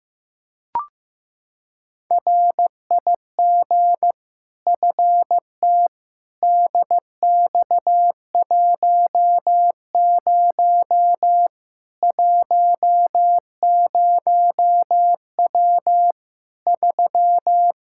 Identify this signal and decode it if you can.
Morse code, heard as a keyed tone, about 15 wpm, 700 Hz. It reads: RIG FT DX10 10W 3
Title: K1USN SST QSO Practice 15wpm